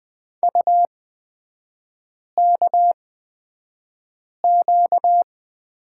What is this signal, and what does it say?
Morse code, heard as a keyed tone, about 20 wpm, spaced as for 5 wpm, 700 Hz. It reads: UKQ
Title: 255 Lesson 41e - Period - Three Characters 20wpm